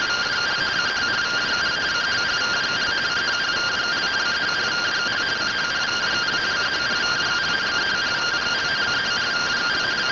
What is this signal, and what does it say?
Two-tone teleprinter signal, about 170 Hz shift.